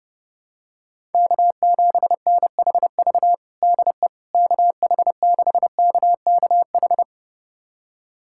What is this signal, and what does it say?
Morse code, heard as a keyed tone, about 30 wpm, 700 Hz. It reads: K7NHV DE KH6KKH
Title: USA Callsign DE Callsign 30wpm